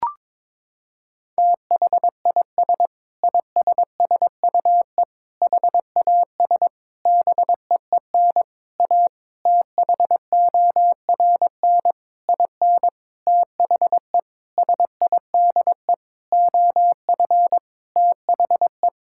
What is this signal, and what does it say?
Morse code, heard as a keyed tone, about 21 wpm, 700 Hz. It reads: THIS ISSUE HAS BEEN A THORN IN THE SIDE OF THE
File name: practice / 22wpm / Sentences with 8-Word Phrases - 22wpm